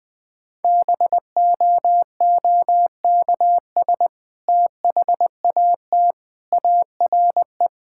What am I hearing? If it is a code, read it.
Morse code, heard as a keyed tone, about 20 wpm, 700 Hz. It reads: BOOKS THAT ARE
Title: Sentences with 2-Word Phrases - Spoken Beginning - 20wpm